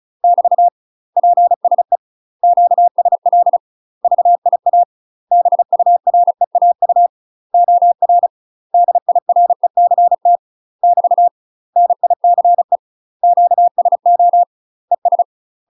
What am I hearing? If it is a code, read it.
Morse code, heard as a keyed tone, about 35 wpm, 700 Hz. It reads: <BT> PSE QSL VIA BUREAU OR DIRECT <BT> NICE QSO ES
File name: UK MASTER QSOs TRAINING / 35WPM / UK MASTER QSOs 0018